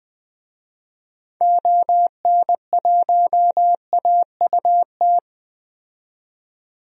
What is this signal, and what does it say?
Morse code, heard as a keyed tone, about 20 wpm, 700 Hz. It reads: ON1AUT
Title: DXCC-100 Call Signs - 6 Characters - 20wpm